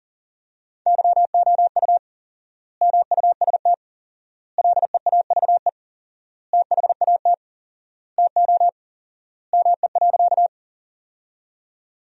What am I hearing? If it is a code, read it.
Morse code, heard as a keyed tone, about 37 wpm, 700 Hz. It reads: YOU MUST LEAVE THAT TO ME.
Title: 4x Word-Spacing - Sentences from Top 400 Words - 40wpm